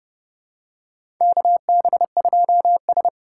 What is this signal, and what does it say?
Morse code, heard as a keyed tone, about 30 wpm, 700 Hz. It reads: KB2S
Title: K1USN SST QSO Practice - 30wpm